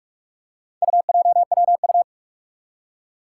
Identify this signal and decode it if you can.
Morse code, heard as a keyed tone, about 46 wpm, 700 Hz. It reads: UJWU
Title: Four Challenging Letter-Number Combinations 45wpm